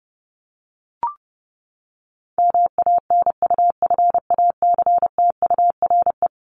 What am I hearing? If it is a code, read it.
Morse code, heard as a keyed tone, about 30 wpm, 700 Hz. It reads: MANUFACTURE
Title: ING Suffix - Top 100 Words with Base Words 30wpm